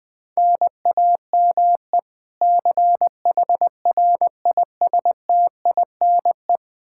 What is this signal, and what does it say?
Morse code, heard as a keyed tone, about 21 wpm, 700 Hz. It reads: NAME CHRISTINE
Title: Practice QSO - Exchange 1 - Beginning - 20wpm